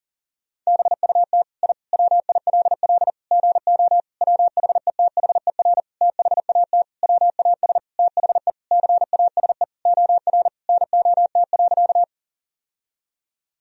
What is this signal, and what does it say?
Morse code, heard as a keyed tone, about 40 wpm, 700 Hz. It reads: BUT I WILL GO WHETHER THAT WAS THE CASE OR NOT.
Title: Sentences from Top 1000 Words - Review Part 3 - 40wpm